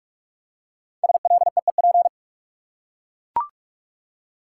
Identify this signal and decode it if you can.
Morse code, heard as a keyed tone, about 42 wpm, 700 Hz. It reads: SLEEP
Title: Five Letter Words - 45wpm